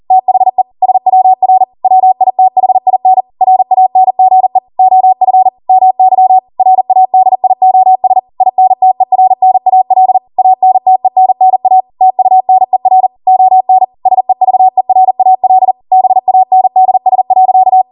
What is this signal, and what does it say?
Morse code, heard as a keyed tone, about 40 wpm, 750 Hz. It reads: THE SWR WITHIN RANGE OF MY RADIOS INTERNAL ANTENNA TUNER ON SEVERAL BANDS.